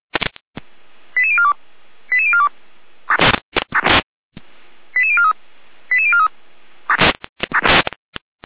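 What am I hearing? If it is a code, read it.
Five-tone selective call (ZVEI), heard as quick runs of five tones, tones of about 70 ms. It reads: 88042 88042 88043 88043
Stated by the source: Received on an SDR as a ZVEI five-tone call, code 88189.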